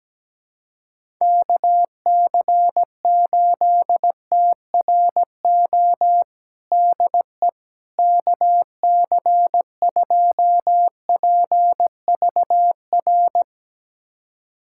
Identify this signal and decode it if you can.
Morse code, heard as a keyed tone, about 17 wpm, 700 Hz. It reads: KC8TRO DE KC2PVR
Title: USA Callsign DE Callsign - 6 Characters - 17wpm